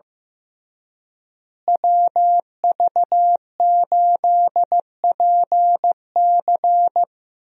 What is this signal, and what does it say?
Morse code, heard as a keyed tone, about 15 wpm, 700 Hz. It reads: WV8PC
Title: USA Call Signs - 5 Characters - Repeated 3x - 15wpm